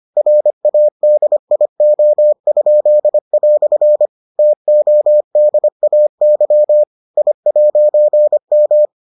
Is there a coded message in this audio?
Yes: Morse code from 0.2 to 8.8 s, heard as a keyed tone, about 25 wpm, 600 Hz. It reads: RADIO?" TODAY I'M